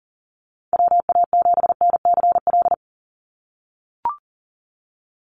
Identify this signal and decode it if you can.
Morse code, heard as a keyed tone, about 40 wpm, 700 Hz. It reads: WA7NCL
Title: NA SOTA Activator Callsigns 40wpm